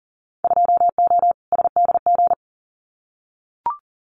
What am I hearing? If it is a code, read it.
Morse code, heard as a keyed tone, about 40 wpm, 700 Hz. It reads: 2O SDG
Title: Winter Field Day Exchanges 40wpm